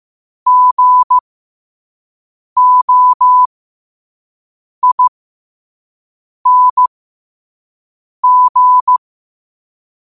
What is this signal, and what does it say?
Morse code, heard as a keyed tone, about 15 wpm, spaced as for 5 wpm, 1000 Hz. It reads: GOING
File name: qotdfiles / 5wpm